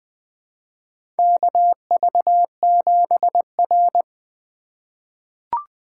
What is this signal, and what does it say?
Morse code, heard as a keyed tone, about 20 wpm, 700 Hz. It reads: KV7R
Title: USA Call Signs - 4 Characters - 4x Character Spacing - 20wpm